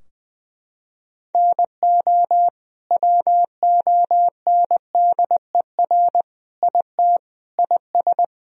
Morse code, heard as a keyed tone, about 20 wpm, 700 Hz. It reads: NO WONDER IT IS
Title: Sentences from Top 1000 Words - Review Part 3 - 20wpm